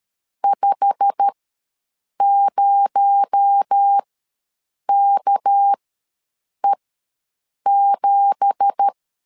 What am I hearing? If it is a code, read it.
Morse code, heard as a keyed tone, about 13 wpm, spaced as for 7 wpm, 800 Hz. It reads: 50KE7
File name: LEZIONI / 9 2867 ws 12wpm 3431